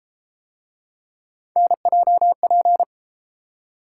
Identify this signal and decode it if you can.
Morse code, heard as a keyed tone, about 31 wpm, 700 Hz. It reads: NJP